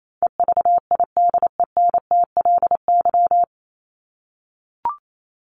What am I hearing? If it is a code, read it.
Morse code, heard as a keyed tone, about 28 wpm, 700 Hz. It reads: EVIDENTLY
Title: LY Suffix - Top 100 Words with Base Words 28wpm